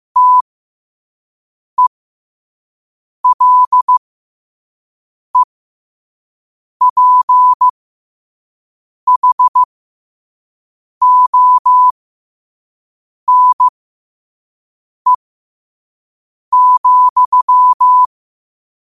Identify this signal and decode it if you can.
Morse code, heard as a keyed tone, about 15 wpm, spaced as for 5 wpm, 1000 Hz. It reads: TELEPHONE,
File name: qotdfiles / 5wpm